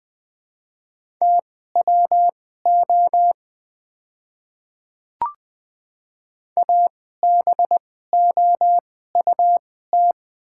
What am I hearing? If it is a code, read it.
Morse code, heard as a keyed tone, about 20 wpm, spaced as for 15 wpm, 690 Hz. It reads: TWO ABOUT
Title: 2x Character Spacing - Top 100 Words - 20wpm